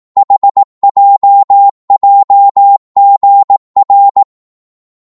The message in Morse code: HJJGR